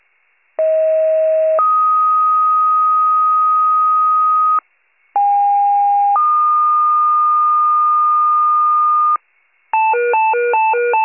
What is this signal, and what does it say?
Two-tone page, heard as a one-second tone, then a three-second tone, about 640 then 1250 Hz, about 810 then 1220 Hz. It is heard from 0.6 to 9.2 s.